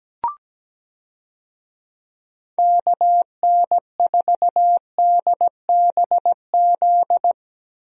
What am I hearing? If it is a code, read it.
Morse code, heard as a keyed tone, about 17 wpm, 700 Hz. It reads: KN4DBZ
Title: USA Call Signs - 6 Characters - Repeated 2x - 17wpm